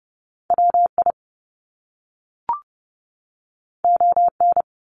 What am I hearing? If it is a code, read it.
Morse code, heard as a keyed tone, about 30 wpm, 700 Hz. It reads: WI ON
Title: Top Two Letter Combinations in Words 30wpm